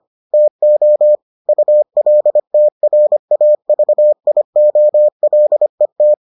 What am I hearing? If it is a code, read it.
Morse code, heard as a keyed tone, about 25 wpm, 600 Hz. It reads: TO ULTRAVIOLET